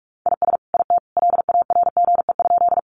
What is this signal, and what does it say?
Morse code, heard as a keyed tone, about 44 wpm, 700 Hz. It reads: IS IT LARGE?
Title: Sentences from Top 200 Words 45wpm